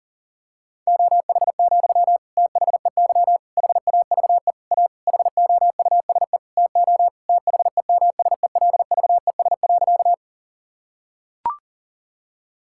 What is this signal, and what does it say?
Morse code, heard as a keyed tone, about 40 wpm, 700 Hz. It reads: OH, THEY HAVE A HOUSE TO THEMSELVES.
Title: Sentences from Top 1000 Words - Review Part 5 - 40wpm